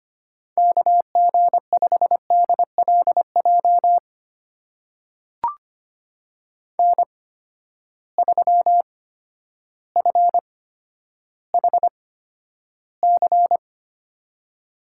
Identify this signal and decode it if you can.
Morse code, heard as a keyed tone, about 25 wpm, spaced as for 11 wpm, 700 Hz. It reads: KG5DLJ N3FHC
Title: USA Call Signs - 8x Character Spacing - 25wpm